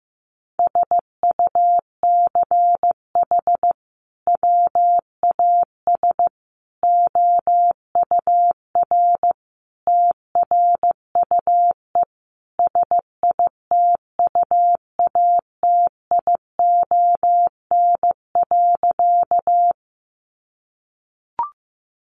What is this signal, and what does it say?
Morse code, heard as a keyed tone, about 15 wpm, 700 Hz. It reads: SUCH WAS OUR TRUE SITUATION.
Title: Sentences from Top 1200 Words 15wpm